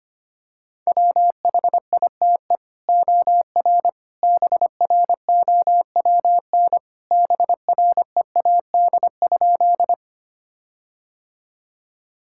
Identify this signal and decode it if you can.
Morse code, heard as a keyed tone, about 25 wpm, 700 Hz. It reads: WHITE OR BROWN BREAD?
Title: Sentences from Top 1600 Words 25wpm